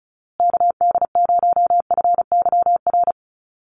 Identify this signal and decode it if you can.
Morse code, heard as a keyed tone, about 34 wpm, 700 Hz. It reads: KD0FYR
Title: USA Call Signs Repeated 2x 35wpm